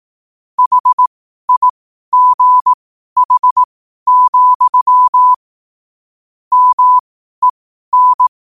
Morse code, heard as a keyed tone, about 18 wpm, spaced as for 12 wpm, 1000 Hz. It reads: HIGH, MEN
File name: qotdfiles / 12wpm